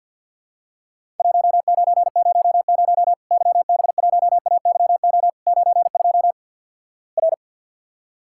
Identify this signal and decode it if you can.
Morse code, heard as a keyed tone, about 48 wpm, 700 Hz. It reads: 1900 YB1AYO 92 R
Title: Worked All Europe DX Contest - QTC Exchange with Repeats - 50wpm